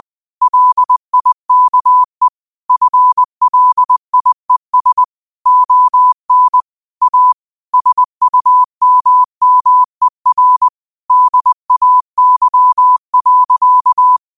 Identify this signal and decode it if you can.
Morse code, heard as a keyed tone, about 20 wpm, 1000 Hz. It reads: LIKE FLIES ON A SUMMER DAY.
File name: qotdfiles / 20wpm